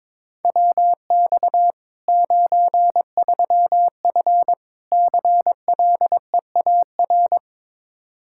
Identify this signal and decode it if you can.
Morse code, heard as a keyed tone, about 23 wpm, 700 Hz. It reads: WX 93F CLEAR
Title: Practice QSO - Exchange 2 - Info - 22wpm